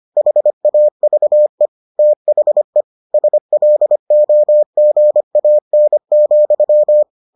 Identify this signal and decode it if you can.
Morse code, heard as a keyed tone, about 25 wpm, 600 Hz. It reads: HAVE THE SLOGAN,